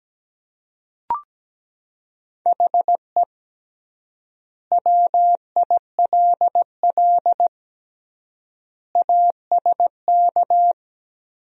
Morse code, heard as a keyed tone, about 17 wpm, 700 Hz. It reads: HE WILL ASK
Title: 3x Word-Spacing - Sentences from Top 200 Words - 17wpm